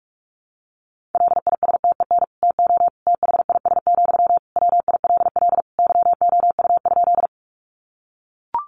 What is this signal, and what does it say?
Morse code, heard as a keyed tone, about 44 wpm, 700 Hz. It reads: LISTEN TO THIS, WILL YOU?
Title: Sentences from Top 400 Words 45wpm